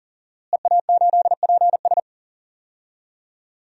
Morse code, heard as a keyed tone, about 40 wpm, 700 Hz. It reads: EA8PS
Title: DXCC-100 Call Signs - 5 Characters - Repeated 3x - 40wpm